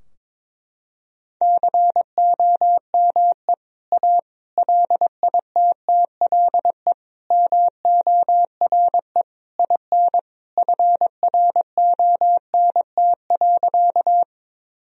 Morse code, heard as a keyed tone, about 22 wpm, 700 Hz. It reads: COME A LITTLE MORE IN FRONT.
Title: Sentences from Top 1000 Words - Review Part 2 - 22wpm